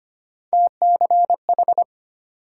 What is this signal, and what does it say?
Morse code, heard as a keyed tone, about 34 wpm, 700 Hz. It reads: TCH